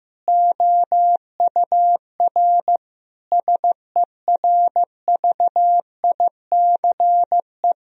Morse code, heard as a keyed tone, about 15 wpm, 700 Hz. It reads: OUR SERVICE
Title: Sentences from Top 1000 Words - Review Part 4 - 15wpm